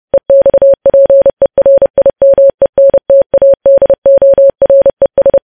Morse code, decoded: EXPERIMENTADORES